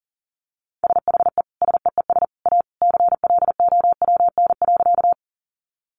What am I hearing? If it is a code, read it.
Morse code, heard as a keyed tone, about 40 wpm, 700 Hz. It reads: SHE SEES A CLOWN.
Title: Easy Phrases 40wpm